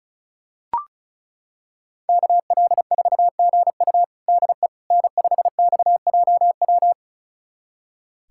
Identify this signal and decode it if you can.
Morse code, heard as a keyed tone, about 36 wpm, 700 Hz. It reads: KL4GU DE N5XJW
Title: USA Callsign DE Callsign - 5 Characters - 35wpm